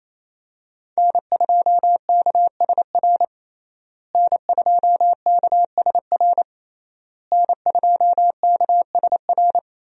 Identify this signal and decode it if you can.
Morse code, heard as a keyed tone, about 27 wpm, 700 Hz. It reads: N2KSR N2KSR N2KSR
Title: USA Call Signs - 5 Characters - Repeated 3x - 28wpm